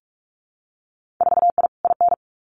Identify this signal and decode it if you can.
Morse code, heard as a keyed tone, about 44 wpm, 700 Hz. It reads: 4I IN
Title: Winter Field Day Exchanges 45wpm